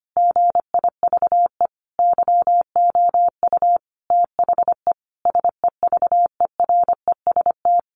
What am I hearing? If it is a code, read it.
Morse code, heard as a keyed tone, about 25 wpm, 700 Hz. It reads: GIVE YOU THE SEVEREST
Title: A Sound Of Thunder 25wpm